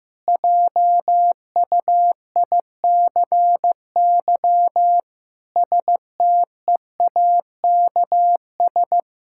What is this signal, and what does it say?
Morse code, heard as a keyed tone, about 15 wpm, 700 Hz. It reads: JUICY STEAKS
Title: Pangram Challenge v2 - 15wpm